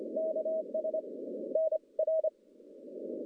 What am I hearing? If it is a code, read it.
Morse code, heard as a keyed tone, about 25 wpm, 640 Hz, with noise. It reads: KS N R
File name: contests / CW-SS-k5tr- -1200